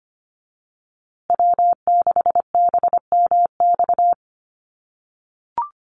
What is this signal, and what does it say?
Morse code, heard as a keyed tone, about 25 wpm, 700 Hz. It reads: W6BMX
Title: USA Call Signs Repeated 3x 25wpm